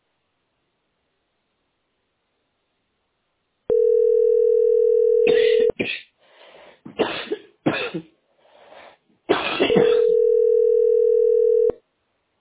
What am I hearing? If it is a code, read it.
Ringback tone (North America), 440 + 480 Hz.